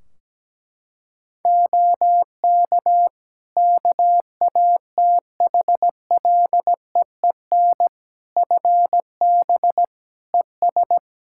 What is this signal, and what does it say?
Morse code, heard as a keyed tone, about 17 wpm, 700 Hz. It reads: OK KATHLEEN FB ES
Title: Practice QSO - Exchange 2 - Info - 17wpm